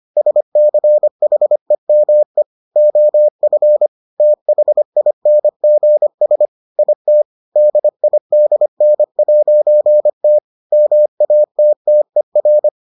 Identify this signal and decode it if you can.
Morse code, heard as a keyed tone, about 25 wpm, 600 Hz. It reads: SCHEME OF THINGS IT DIDN'T MATTER